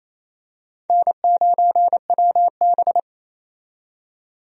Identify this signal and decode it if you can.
Morse code, heard as a keyed tone, about 28 wpm, 700 Hz. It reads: N9WB
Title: USA Call Signs - 4 Characters - 28wpm